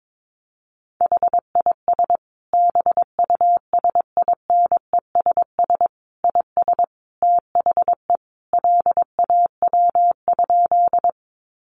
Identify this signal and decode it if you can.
Morse code, heard as a keyed tone, about 22 wpm, 700 Hz. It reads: HIS BUSINESS IS THE LAW?
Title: Sentences from Top 600 Words 22wpm